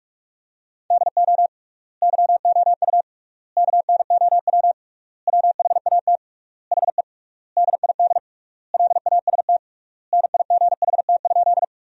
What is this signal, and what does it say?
Morse code, heard as a keyed tone, about 47 wpm, 700 Hz. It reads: DO YOU KNOW WHAT HE DID LAST NIGHT?
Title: 3x Word-Spacing - Sentences from Top 300 Words - 45wpm